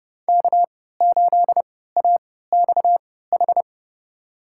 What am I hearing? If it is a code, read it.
Morse code, heard as a keyed tone, about 30 wpm, spaced as for 17 wpm, 700 Hz. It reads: K8AXH
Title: USA Call Signs - 5 Characters - 3x Character Spacing - 30wpm